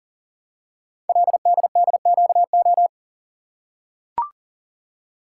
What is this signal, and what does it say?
Morse code, heard as a keyed tone, about 38 wpm, 700 Hz. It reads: LDDQO
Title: Code Groups - 5 Letters - 40wpm